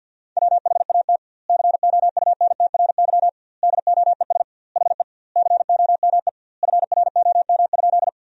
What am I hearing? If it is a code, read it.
Morse code, heard as a keyed tone, about 50 wpm, 700 Hz. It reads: WHAT COUNTRY DOES HE COME FROM?
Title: 2x Word-Spacing - Sentences from Top 300 Words - 50wpm